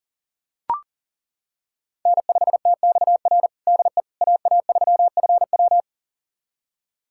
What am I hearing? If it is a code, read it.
Morse code, heard as a keyed tone, about 40 wpm, 700 Hz. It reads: N5TXR DE AA3FW
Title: USA Callsign DE Callsign - 5 Characters - 40wpm